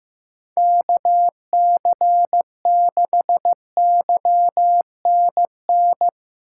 Morse code, heard as a keyed tone, about 15 wpm, 700 Hz. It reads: KC6YNN
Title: USA Call Signs - 6 Characters - 3x Character Spacing - 15wpm